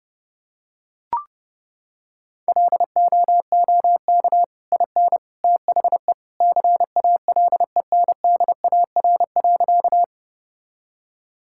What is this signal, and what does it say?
Morse code, heard as a keyed tone, about 30 wpm, 700 Hz. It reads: LOOK IN THE CALENDAR.
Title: Sentences from Top 1600 Words 30wpm